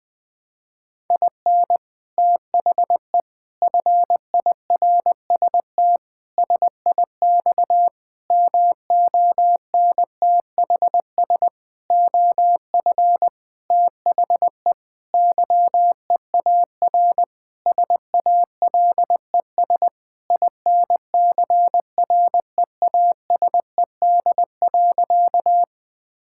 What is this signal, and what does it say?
Morse code, heard as a keyed tone, about 21 wpm, 700 Hz. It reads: IN THE FIRST SIX MONTHS OF THE YEAR SALES INCREASED.
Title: Sentences with 6-Word Phrases - 20wpm